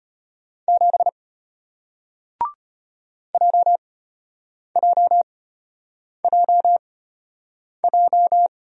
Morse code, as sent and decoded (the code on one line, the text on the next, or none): --... / .--- .--- .--- .---
7 JJJJ